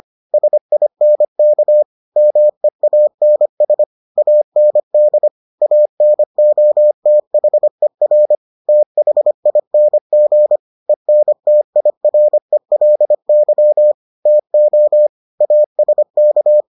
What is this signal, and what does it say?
Morse code, heard as a keyed tone, about 25 wpm, 600 Hz. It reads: SINK MEANS AND ANOTHER THING ENTIRELY TO ASK